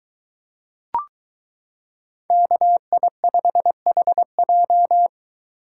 Morse code, heard as a keyed tone, about 23 wpm, 700 Hz. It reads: KI5HJ